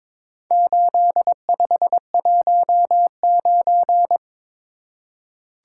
Morse code, heard as a keyed tone, about 22 wpm, 690 Hz. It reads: 8519
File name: practice / 22wpm / Four Letter-Number Combinations - 22wpm